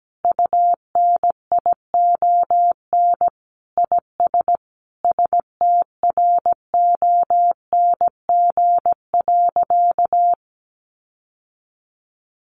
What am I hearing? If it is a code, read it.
Morse code, heard as a keyed tone, about 17 wpm, 700 Hz. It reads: UNION IS STRONG.
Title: Sentences from Top 1400 Words 17wpm